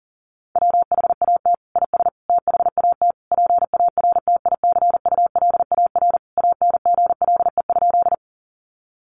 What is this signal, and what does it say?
Morse code, heard as a keyed tone, about 40 wpm, 700 Hz. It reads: WHAT IS THAT PARTICULAR ANGLE?
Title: Sentences from Top 1500 Words 40wpm